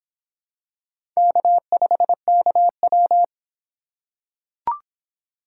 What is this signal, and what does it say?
Morse code, heard as a keyed tone, about 26 wpm, 700 Hz. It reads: K5KW